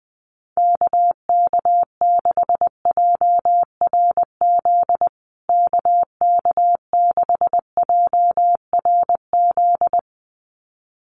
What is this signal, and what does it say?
Morse code, heard as a keyed tone, about 20 wpm, 700 Hz. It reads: KK6JRZ KK6JRZ
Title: USA Call Signs Repeated 2x 20wpm